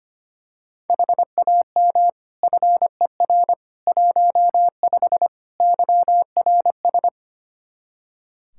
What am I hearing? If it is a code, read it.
Morse code, heard as a keyed tone, about 25 wpm, 700 Hz. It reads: HAM FER 15 YRS
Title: Practice QSO - Exchange 3 - More Info - 25wpm